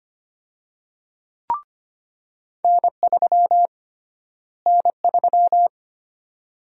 Morse code, heard as a keyed tone, about 25 wpm, 700 Hz. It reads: N3 N3
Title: USA Call Sign Prefixes - Repeated 2x - 25wpm